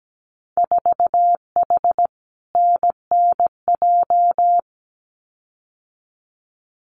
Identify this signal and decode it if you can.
Morse code, heard as a keyed tone, about 17 wpm, 700 Hz. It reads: 4H NNJ
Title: Winter Field Day Exchanges Encore 17wpm